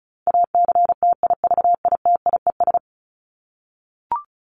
Morse code, heard as a keyed tone, about 34 wpm, 700 Hz. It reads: ACTIVITIES
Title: ES:S Suffix - Top 100 Words with Base Words 35wpm